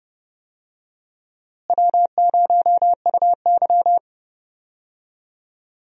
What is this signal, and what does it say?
Morse code, heard as a keyed tone, about 32 wpm, 700 Hz. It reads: W0UY
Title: USA Call Signs - 4 Characters - Repeated 3x - 30wpm